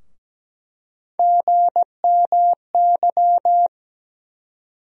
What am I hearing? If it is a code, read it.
Morse code, heard as a keyed tone, about 17 wpm, 700 Hz. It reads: GMY
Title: USA Call Sign Postfixes - 3 Letters 17wpm